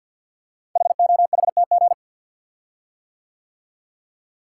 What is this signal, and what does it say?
Morse code, heard as a keyed tone, about 46 wpm, 700 Hz. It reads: HOHTG
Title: Code Groups - 5 Letters - 50wpm